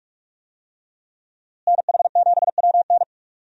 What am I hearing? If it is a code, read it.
Morse code, heard as a keyed tone, about 43 wpm, 700 Hz. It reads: NH7WN
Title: USA Call Signs - 5 Characters - Repeated 2x - 45wpm